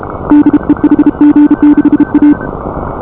Morse code, heard as a keyed tone, about 33 wpm, 310 Hz. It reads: DE HG6A